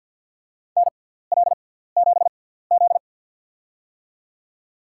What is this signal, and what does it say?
Morse code, heard as a keyed tone, about 53 wpm, spaced as for 17 wpm, 700 Hz. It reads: NL7Z
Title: USA Call Signs - 6x Character Spacing - 50wpm